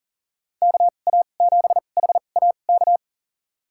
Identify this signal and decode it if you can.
Morse code, heard as a keyed tone, about 40 wpm, spaced as for 29 wpm, 700 Hz. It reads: KA7HAK
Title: USA Call Signs - 2x Character Spacing - 40wpm